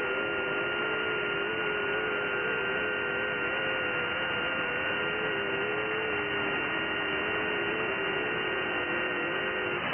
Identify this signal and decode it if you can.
FT8.
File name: IC-7100